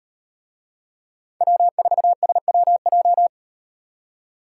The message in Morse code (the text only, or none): W4SWJ